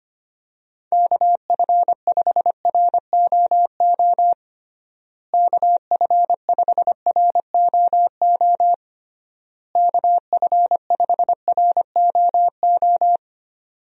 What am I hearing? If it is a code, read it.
Morse code, heard as a keyed tone, about 25 wpm, 700 Hz. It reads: KF5ROO KF5ROO KF5ROO
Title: USA Call Signs - 6 Characters - Repeated 3x - 25wpm